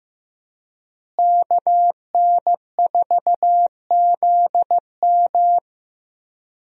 Morse code, heard as a keyed tone, about 15 wpm, 700 Hz. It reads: KN4ZM
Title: USA Call Signs - 5 Characters - Repeated 2x - 15wpm